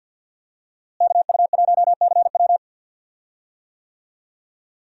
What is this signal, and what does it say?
Morse code, heard as a keyed tone, about 46 wpm, 700 Hz. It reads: KU1CW
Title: USA Call Signs - 5 Characters - Repeated 3x - 50wpm